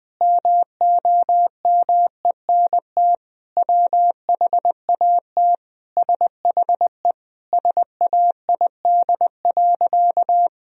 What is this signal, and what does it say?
Morse code, heard as a keyed tone, about 20 wpm, 700 Hz. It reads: MOMENT WHAT SHE SAID.
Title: Sentences from Top 1000 Words - Review Part 4 - 20wpm